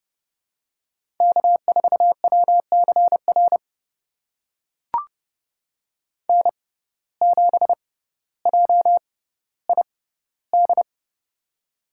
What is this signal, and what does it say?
Morse code, heard as a keyed tone, about 32 wpm, spaced as for 16 wpm, 700 Hz. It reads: K4WCR N7JID
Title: USA Call Signs - 6x Character Spacing - 30wpm